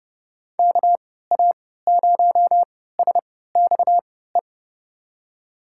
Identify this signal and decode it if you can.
Morse code, heard as a keyed tone, about 30 wpm, spaced as for 17 wpm, 700 Hz. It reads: KA0SXE
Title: USA Call Signs - 6 Characters - 3x Character Spacing - 30wpm